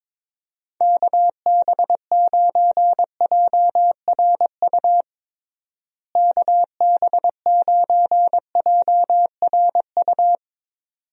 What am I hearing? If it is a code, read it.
Morse code, heard as a keyed tone, about 22 wpm, 700 Hz. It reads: KB9JRU KB9JRU
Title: USA Call Signs - 6 Characters - Repeated 2x - 22wpm